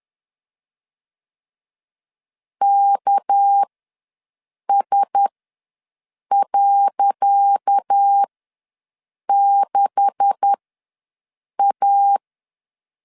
Morse code, heard as a keyed tone, about 10 wpm, spaced as for 6 wpm, 800 Hz. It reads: KS.6A
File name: LEZIONI / 10 SIGNS / 10C DOT ws 10wpm 4213